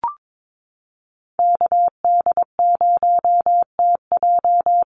Morse code, heard as a keyed tone, about 22 wpm, 700 Hz. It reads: KD0TJ
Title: USA Call Signs 22wpm